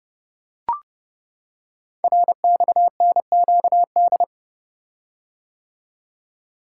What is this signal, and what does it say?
Morse code, heard as a keyed tone, about 29 wpm, 700 Hz. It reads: RXNQD
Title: Code Groups - 5 Letters - 30wpm